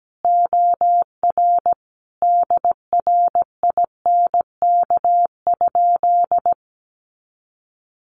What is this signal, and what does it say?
Morse code, heard as a keyed tone, about 17 wpm, 700 Hz. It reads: OR DRINK?
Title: Sentences from Top 1500 Words 17wpm